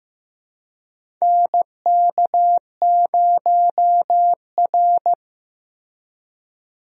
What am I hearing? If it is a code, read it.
Morse code, heard as a keyed tone, about 15 wpm, 700 Hz. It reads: NK0R
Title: USA Call Signs - 4 Characters - Repeated 2x - 15wpm